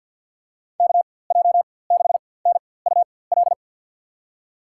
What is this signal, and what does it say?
Morse code, heard as a keyed tone, about 48 wpm, spaced as for 23 wpm, 700 Hz. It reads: KJ6NUL
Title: USA Call Signs - 6 Characters - 4x Character Spacing - 50wpm